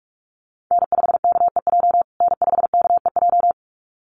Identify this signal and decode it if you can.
Morse code, heard as a keyed tone, about 44 wpm, 700 Hz. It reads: N5KEJ N5KEJ
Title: USA Call Signs Repeated 2x 45wpm